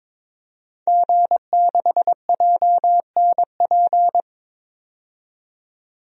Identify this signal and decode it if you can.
Morse code, heard as a keyed tone, about 22 wpm, 700 Hz. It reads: G6JNP
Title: DXCC-100 Call Signs - 5 Characters - 22wpm